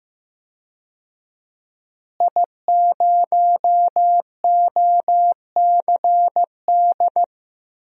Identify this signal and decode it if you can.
Morse code, heard as a keyed tone, about 16 wpm, 700 Hz. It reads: I0OCD
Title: DXCC-100 Call Signs - 5 Characters - Repeated 2x - 15wpm